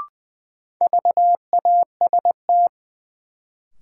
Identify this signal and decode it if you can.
Morse code, heard as a keyed tone, about 21 wpm, 700 Hz. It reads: VAST
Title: Sentences from Top 1300 Words - New Word Drill - 20wpm